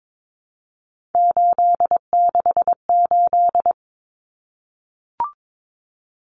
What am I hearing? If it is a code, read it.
Morse code, heard as a keyed tone, about 22 wpm, 700 Hz. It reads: 868
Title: Numbers - 3-Digit 22wpm